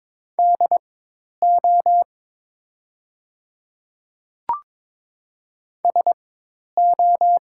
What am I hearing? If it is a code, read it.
Morse code, heard as a keyed tone, about 22 wpm, spaced as for 10 wpm, 700 Hz. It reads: DO SO
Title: 4x Character Spacing - Top 100 Words - 22wpm